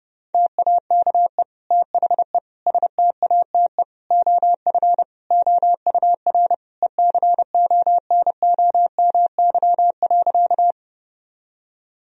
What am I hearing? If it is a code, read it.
Morse code, heard as a keyed tone, about 30 wpm, 700 Hz. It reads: TAKE THE STATE OF OUR ECONOMY.
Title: Sentences from Top 1000 Words - Review Part 3 - 30wpm